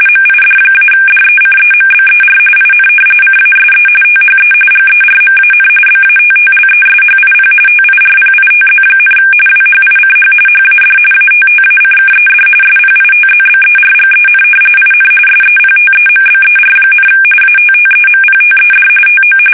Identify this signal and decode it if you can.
RTTY, about 45.45 baud. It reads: MID-DECEMBER IN CASE THE GERMAN BATTLESHIP TIRPITZ, WHICH HAD ATTACKED SPITZBERGEN IN SEPTEMBER, ATTEMPTED TO
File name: fsk850 / battleshipnj